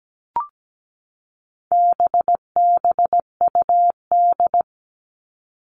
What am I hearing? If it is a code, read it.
Morse code, heard as a keyed tone, about 17 wpm, 700 Hz. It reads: BBUD
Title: All Cut Numbers - 4-Digit 17wpm